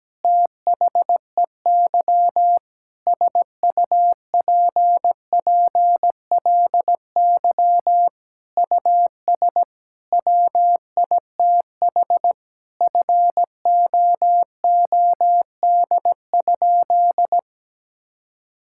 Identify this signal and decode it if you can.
Morse code, heard as a keyed tone, about 17 wpm, 700 Hz. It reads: THEY SUPPLY US WITH FOOD?
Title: Sentences from Top 1000 Words - Review Part 5 - 17wpm